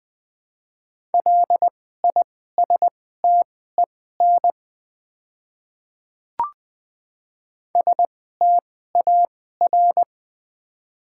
Morse code, heard as a keyed tone, about 20 wpm, spaced as for 14 wpm, 700 Hz. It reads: LISTEN STAR
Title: 2x Character Spacing - Top 500 Words - 20wpm